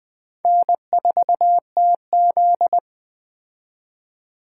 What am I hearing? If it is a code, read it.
Morse code, heard as a keyed tone, about 20 wpm, 700 Hz. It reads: N4TZ
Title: K1USN SST QSO Practice - 20wpm